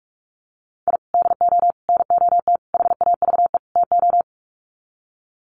Morse code, heard as a keyed tone, about 44 wpm, 700 Hz. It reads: I DO NOT HAVE TO
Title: Sets of 5 Words 45wpm